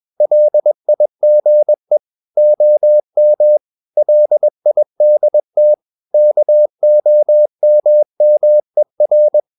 Morse code, heard as a keyed tone, about 21 wpm, 600 Hz. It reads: LIGE OM LIDT KOMMER
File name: Ferie QRS 21wpm